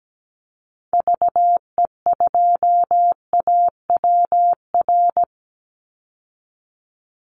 Morse code, heard as a keyed tone, about 17 wpm, 700 Hz. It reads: VE2AWR
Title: Call Signs Repeated 3x - 17wpm